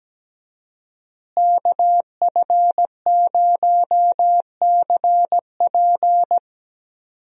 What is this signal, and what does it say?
Morse code, heard as a keyed tone, about 17 wpm, 700 Hz. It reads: KF0CP
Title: USA Call Signs - 5 Characters - Repeated 3x - 17wpm